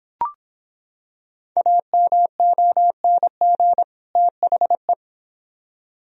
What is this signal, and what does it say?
Morse code, heard as a keyed tone, about 25 wpm, 700 Hz. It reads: AMONG THE